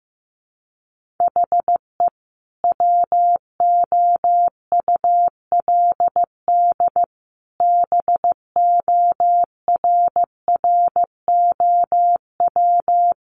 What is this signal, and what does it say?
Morse code, heard as a keyed tone, about 15 wpm, 700 Hz. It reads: HE WOULD BORROW
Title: Sentences from Top 1500 Words 15wpm